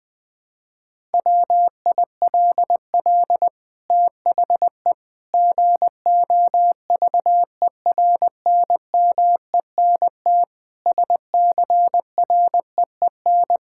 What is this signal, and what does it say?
Morse code, heard as a keyed tone, about 20 wpm, 700 Hz. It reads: WILL THE GOVERNMENT SCREEN
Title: Sentences from Top 1000 Words - Review Part 6 - 20wpm